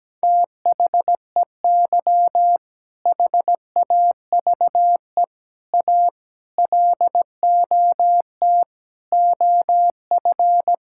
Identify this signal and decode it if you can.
Morse code, heard as a keyed tone, about 17 wpm, 700 Hz. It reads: THEY HAVE A LOT OF
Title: Sentences from Top 1000 Words - Review Part 5 - 17wpm